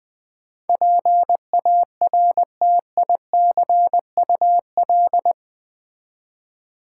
Morse code, heard as a keyed tone, about 20 wpm, 700 Hz. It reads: PARTICUL
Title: Build A Word - 4 Letters At A Time - 12-Letter Words - 20wpm